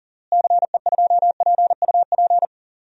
Morse code, heard as a keyed tone, about 39 wpm, 700 Hz. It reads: CE2PUP
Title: DXCC-100 Call Signs - 6 Characters - Repeated 3x - 40wpm